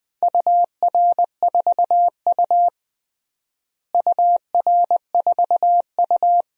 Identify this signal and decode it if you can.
Morse code, heard as a keyed tone, about 21 wpm, 700 Hz. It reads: UR4U UR4U
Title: DXCC-100 Call Signs - 4 Characters - Repeated 2x - 20wpm